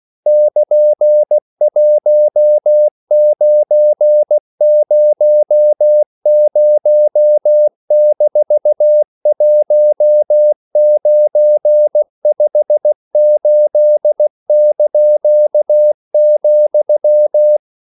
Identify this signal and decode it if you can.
Morse code, heard as a keyed tone, about 16 wpm, 600 Hz. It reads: <KN>1900-1958),
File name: CW / Practice / EnglishQuotations0416